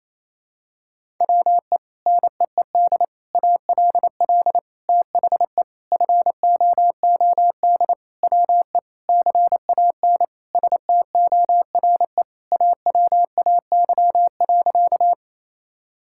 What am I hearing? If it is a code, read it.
Morse code, heard as a keyed tone, about 28 wpm, 700 Hz. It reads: WE NEED ALL THE FOOD WE CAN STORE AWAY.
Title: Sentences from Top 1000 Words - Review Part 5 - 28wpm